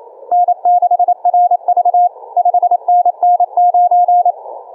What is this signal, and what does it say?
Morse code, heard as a keyed tone, about 27 wpm, 710 Hz. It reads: N6RV 5NN9
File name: kf6mwa / cqww cw2